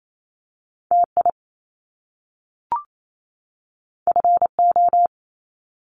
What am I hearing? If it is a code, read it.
Morse code, heard as a keyed tone, about 28 wpm, 700 Hz. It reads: TI FO
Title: Top Two Letter Combinations in Words 28wpm